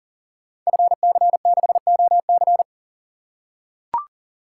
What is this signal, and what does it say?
Morse code, heard as a keyed tone, about 40 wpm, 700 Hz. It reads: FC6OC
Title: Code Groups - 5 Letter:Numbers - 40wpm